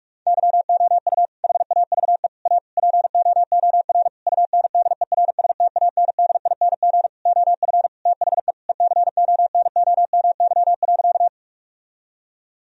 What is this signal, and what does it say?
Morse code, heard as a keyed tone, about 46 wpm, 700 Hz. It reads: YOU HAVE A POOR UNDERSTANDING OF THE ECONOMY.
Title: Sentences from Top 1000 Words - Review Part 3 - 45wpm